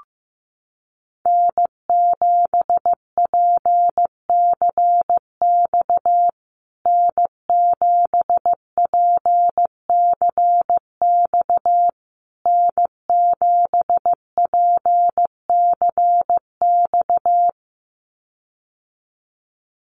Morse code, heard as a keyed tone, about 15 wpm, 700 Hz. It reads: N7PCX N7PCX N7PCX
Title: USA Call Signs Repeated 3x 15wpm